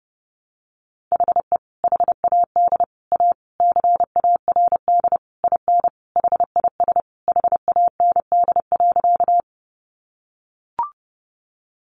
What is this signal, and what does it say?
Morse code, heard as a keyed tone, about 30 wpm, 700 Hz. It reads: HE HAD A CARD IN HIS HAND.
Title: Sentences from Top 600 Words 30wpm